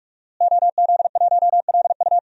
Morse code, heard as a keyed tone, about 46 wpm, 700 Hz. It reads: OZ1LU
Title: DXCC-100 Call Signs - 5 Characters - Repeated 2x - 45wpm